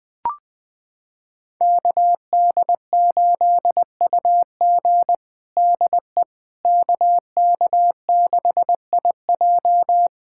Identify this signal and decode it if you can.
Morse code, heard as a keyed tone, about 20 wpm, 700 Hz. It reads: KD8UG DE KK6IJ
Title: USA Callsign DE Callsign - 5 Characters - 20wpm